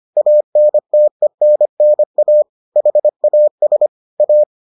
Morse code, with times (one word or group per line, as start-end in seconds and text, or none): 0.2-2.4 ANTENNA
2.8-3.9 HAS
4.2-4.4 A